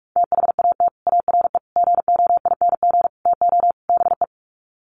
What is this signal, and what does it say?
Morse code, heard as a keyed tone, about 44 wpm, 700 Hz. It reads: THAT ARE GOING TO BE
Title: Sets of 5 Words Encore2 45wpm